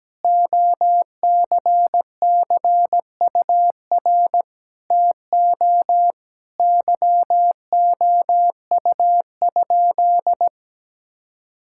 Morse code, heard as a keyed tone, about 17 wpm, 700 Hz. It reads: OCCUR TO YOU?
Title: Sentences from Top 1000 Words - Review Part 6 - 17wpm